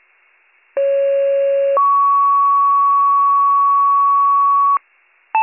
Two-tone page, heard as a one-second tone, then a three-second tone, about 550 then 1120 Hz. It is heard from 0.8 to 4.8 s.